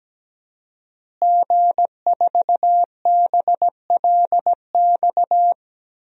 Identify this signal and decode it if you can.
Morse code, heard as a keyed tone, about 17 wpm, 700 Hz. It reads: G4BLX
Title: DXCC-100 Call Signs - 5 Characters - Repeated 2x - 17wpm